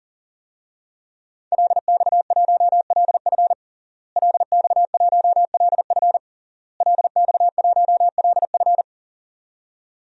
Morse code, heard as a keyed tone, about 40 wpm, 700 Hz. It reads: LX1LF LX1LF LX1LF
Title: DXCC-100 Call Signs - 5 Characters - Repeated 3x - 40wpm